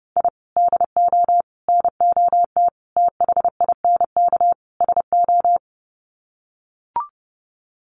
Morse code, heard as a keyed tone, about 30 wpm, 700 Hz. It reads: I DO NOT THINK SO
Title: Sets of 5 Words 30wpm